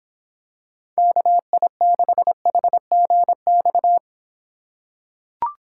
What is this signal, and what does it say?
Morse code, heard as a keyed tone, about 25 wpm, 700 Hz. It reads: KI6HGX